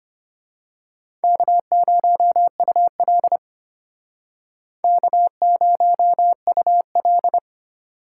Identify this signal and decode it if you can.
Morse code, heard as a keyed tone, about 29 wpm, 700 Hz. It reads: K0UL K0UL
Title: Speed Racing - USA 1x2 Call Signs 25wpm